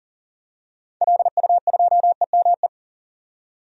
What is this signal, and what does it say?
Morse code, heard as a keyed tone, about 38 wpm, 700 Hz. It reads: LU2EME
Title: DXCC-100 Call Signs - 6 Characters - 40wpm